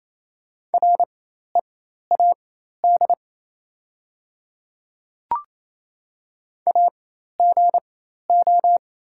Morse code, heard as a keyed tone, about 28 wpm, spaced as for 13 wpm, 690 Hz. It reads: READ AGO
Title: 4x Character Spacing - Top 500 Words - 28wpm